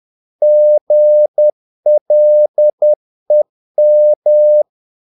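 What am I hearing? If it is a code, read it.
Morse code, heard as a keyed tone, about 10 wpm, 600 Hz. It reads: GLEM
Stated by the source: Morse practice of short Danish words at 10 wpm, from CW med Gnister.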